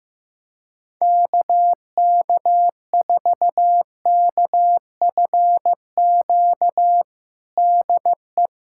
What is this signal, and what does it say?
Morse code, heard as a keyed tone, about 15 wpm, 700 Hz. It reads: KK4KFQ DE
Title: USA Callsign DE Callsign - 6 Characters - 15wpm